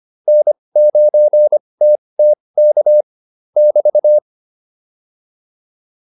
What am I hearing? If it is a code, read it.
Morse code, heard as a keyed tone, about 25 wpm, spaced as for 20 wpm, 600 Hz. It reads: N9TTK <BT>
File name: QSO205 20